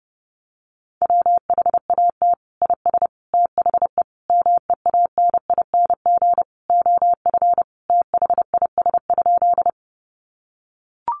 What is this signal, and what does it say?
Morse code, heard as a keyed tone, about 30 wpm, 700 Hz. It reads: WHAT IS THE MEANING OF THIS?
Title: Sentences from Top 900 Words 30wpm